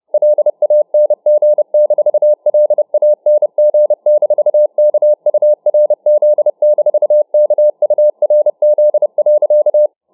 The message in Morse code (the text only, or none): LANG-LANG-KURZ-KURZ.